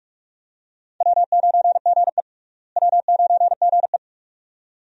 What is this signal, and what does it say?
Morse code, heard as a keyed tone, about 48 wpm, 700 Hz. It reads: W9GE W9GE
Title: USA Call Signs - 4 Characters - Repeated 2x - 45wpm